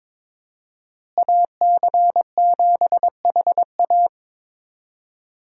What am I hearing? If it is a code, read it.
Morse code, heard as a keyed tone, about 22 wpm, 700 Hz. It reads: AC7HA
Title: USA Call Signs - 5 Characters - Repeated 2x - 22wpm